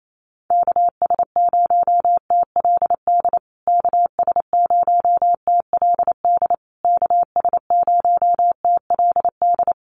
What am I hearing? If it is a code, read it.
Morse code, heard as a keyed tone, about 28 wpm, 700 Hz. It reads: KS0TLD KS0TLD KS0TLD
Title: USA Call Signs Repeated 3x 28wpm